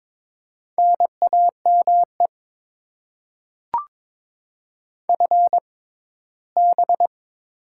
Morse code, heard as a keyed tone, about 21 wpm, spaced as for 15 wpm, 700 Hz. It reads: NAME FB
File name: Top 20 QSO Elements - 6x Character Spacing - 22wpm